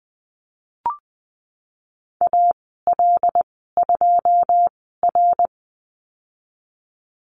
Morse code, heard as a keyed tone, about 20 wpm, spaced as for 15 wpm, 700 Hz. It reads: AL2R
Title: 082 Lesson 14g - Number 2 - Call Signs 20wpm